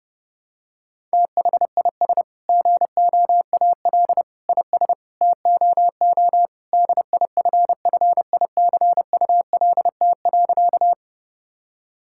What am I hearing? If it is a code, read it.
Morse code, heard as a keyed tone, about 30 wpm, 700 Hz. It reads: THIS GOAL IS TOO DIFFICULT.
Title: Sentences from Top 1000 Words - Review Part 5 - 30wpm